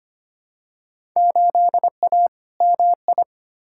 Morse code, heard as a keyed tone, about 25 wpm, 700 Hz. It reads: 8A MI
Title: Field Day Exchanges Encore - 25wpm